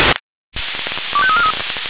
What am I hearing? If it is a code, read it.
Five-tone selective call (ZVEI), heard as quick runs of five tones, tones of about 70 ms. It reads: 25343